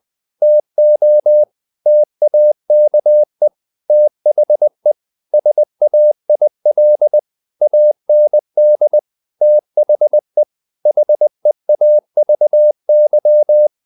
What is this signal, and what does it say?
Morse code, heard as a keyed tone, about 20 wpm, 600 Hz. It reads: TO TAKE THE SAIL AND THE HEAVY
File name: the old man and the sea / the old man and the sea part 1 20wpm